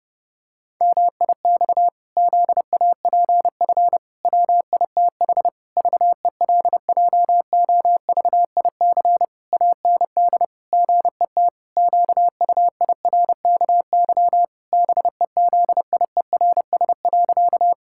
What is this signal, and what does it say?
Morse code, heard as a keyed tone, about 30 wpm, 700 Hz. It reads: MIX ZAPF WITH VELJOVIC AND GET QUIRKY BEZIERS.
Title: Pangram Challenge v2 - 30wpm